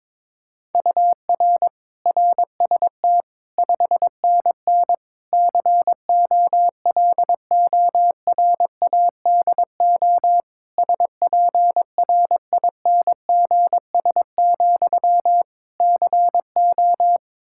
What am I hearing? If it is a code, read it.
Morse code, heard as a keyed tone, about 22 wpm, 700 Hz. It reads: UR RST 5NN COLORADO SPRINGS, CO
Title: Practice QSO - Exchange 1 - Beginning - 22wpm